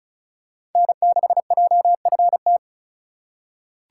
Morse code, heard as a keyed tone, about 36 wpm, 700 Hz. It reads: N6JFT
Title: USA Call Signs - 5 Characters - 4x Character Spacing - 35wpm